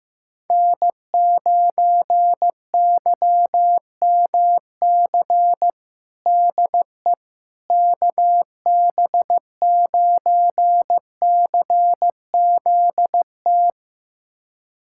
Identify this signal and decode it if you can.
Morse code, heard as a keyed tone, about 15 wpm, 700 Hz. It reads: N9YMC DE KB9CZT
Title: USA Callsign DE Callsign 15wpm